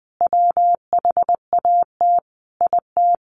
Morse code, heard as a keyed tone, about 20 wpm, 700 Hz. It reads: WHAT IT
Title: Sets of 5 Words 20wpm